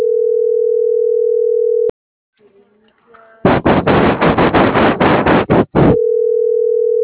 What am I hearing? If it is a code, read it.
Ringback tone (North America), 440 + 480 Hz.